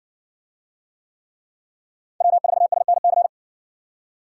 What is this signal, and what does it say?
Morse code, heard as a keyed tone, about 55 wpm, 700 Hz. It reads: W4SNC